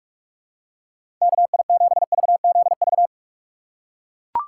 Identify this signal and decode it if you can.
Morse code, heard as a keyed tone, about 41 wpm, 700 Hz. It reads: KI7VZV
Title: USA Call Signs - 6 Characters - 4x Character Spacing - 45wpm